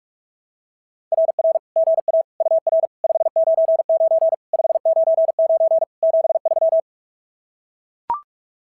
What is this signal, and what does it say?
Morse code, heard as a keyed tone, about 42 wpm, 660 Hz. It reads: RR GA UR 599 599 73
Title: SOTA QSO Practice 45wpm